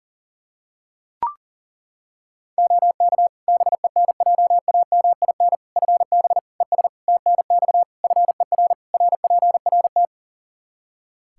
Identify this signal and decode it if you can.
Morse code, heard as a keyed tone, about 40 wpm, 700 Hz. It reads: OK BENJAMIN FB ES TNX FER RPRT
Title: K1USN SST QSO Practice 40wpm